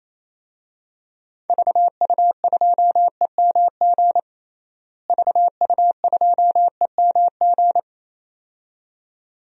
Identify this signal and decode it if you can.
Morse code, heard as a keyed tone, about 28 wpm, 700 Hz. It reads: VU2EMG VU2EMG
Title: DXCC-100 Call Signs - 6 Characters - Repeated 2x - 28wpm